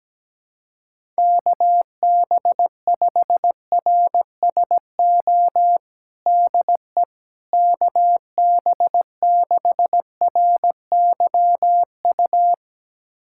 Morse code, heard as a keyed tone, about 17 wpm, 700 Hz. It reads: KB5RSO DE KB6RYU
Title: USA Callsign DE Callsign - 6 Characters - 17wpm